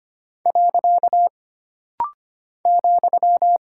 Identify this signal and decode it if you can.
Morse code, heard as a keyed tone, about 25 wpm, 700 Hz. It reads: . ,
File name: Single Punctuation-Prosign - Rapid-Fire - 25wpm